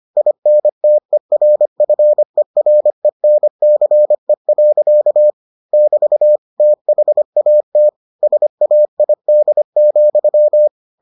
Morse code, heard as a keyed tone, about 25 wpm, 600 Hz. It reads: INTERFERENCE. <BT> THAT SAID,